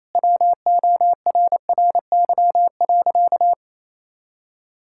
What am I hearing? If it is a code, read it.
Morse code, heard as a keyed tone, about 28 wpm, 700 Hz. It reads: WORRY.
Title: Pangram Challenge v2 - 28wpm